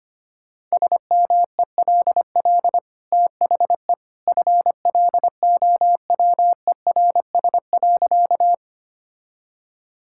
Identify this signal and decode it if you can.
Morse code, heard as a keyed tone, about 25 wpm, 690 Hz. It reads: SMELL THE FLOWERS.
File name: Easy Sentences with Hints - 25wpm